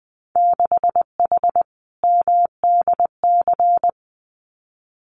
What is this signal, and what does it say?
Morse code, heard as a keyed tone, about 19 wpm, 700 Hz. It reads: 6H MDC
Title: Winter Field Day Exchanges Encore 20wpm